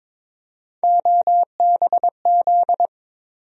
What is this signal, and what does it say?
Morse code, heard as a keyed tone, about 21 wpm, 700 Hz. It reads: OBZ
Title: USA Call Sign Postfixes - 3 Letters 22wpm